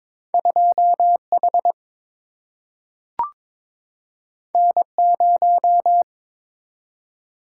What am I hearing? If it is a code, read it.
Morse code, heard as a keyed tone, about 22 wpm, 700 Hz. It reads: 2 H N 0
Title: Single-Letter Single-Number 22wpm